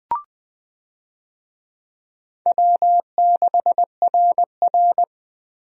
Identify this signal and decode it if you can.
Morse code, heard as a keyed tone, about 20 wpm, 700 Hz. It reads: W6RR
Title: USA Call Signs - 4 Characters - 20wpm